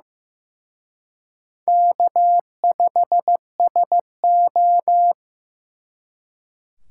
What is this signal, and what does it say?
Morse code, heard as a keyed tone, about 18 wpm, 700 Hz. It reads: K5SO
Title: Speed Racing - USA 1x2 Call Signs 15wpm